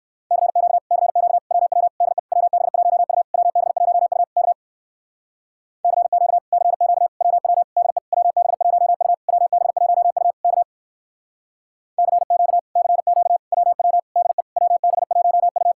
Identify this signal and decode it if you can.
Morse code, heard as a keyed tone, about 66 wpm, 700 Hz. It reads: CQ CQ WW DE WB1U WB1U K CQ CQ WW DE WB1U WB1U K CQ CQ WW DE WB1U